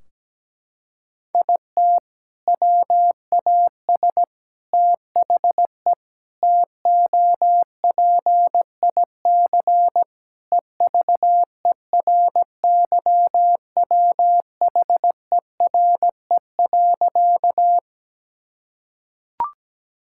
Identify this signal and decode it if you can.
Morse code, heard as a keyed tone, about 17 wpm, 700 Hz. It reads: IT WAS THE TOPIC EVERYWHERE.
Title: Sentences from Top 1600 Words 17wpm